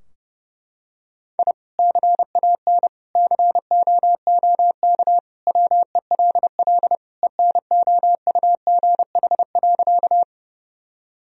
Morse code, heard as a keyed tone, about 32 wpm, 700 Hz. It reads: I CAN COOK WELL ENOUGH.
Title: Sentences from Top 1000 Words - Review Part 4 - 30wpm